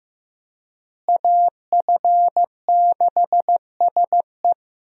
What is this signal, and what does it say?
Morse code, heard as a keyed tone, about 15 wpm, 700 Hz. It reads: AF6SE
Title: USA Call Signs - 5 Characters - Repeated 3x - 15wpm